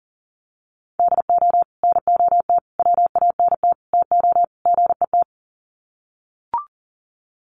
Morse code, heard as a keyed tone, about 38 wpm, 700 Hz. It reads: DO NOT WANT TO GET
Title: Sets of 5 Words 40wpm